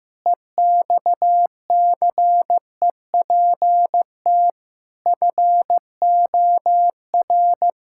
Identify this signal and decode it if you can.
Morse code, heard as a keyed tone, about 15 wpm, 700 Hz. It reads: EXCEPT FOR
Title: Pangram Challenge v2 - 15wpm